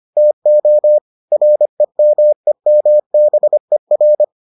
Morse code, heard as a keyed tone, about 25 wpm, 600 Hz. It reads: TO REMEMBER